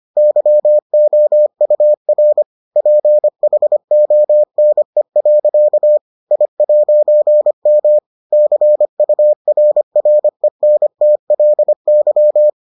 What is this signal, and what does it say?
Morse code, heard as a keyed tone, about 25 wpm, 600 Hz. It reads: YOUR PHONE. I'M CURRENTLY